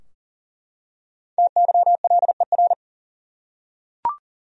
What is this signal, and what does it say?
Morse code, heard as a keyed tone, about 40 wpm, 700 Hz. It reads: TYLER
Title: USA Common Names - 40wpm